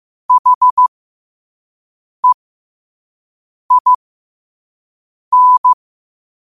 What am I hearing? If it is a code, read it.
Morse code, heard as a keyed tone, about 15 wpm, spaced as for 5 wpm, 1000 Hz. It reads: HEIN
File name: qotdfiles / 5wpm